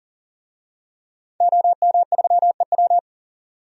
Morse code, heard as a keyed tone, about 40 wpm, 700 Hz. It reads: OM3EW
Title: DXCC-100 Call Signs - 5 Characters - 40wpm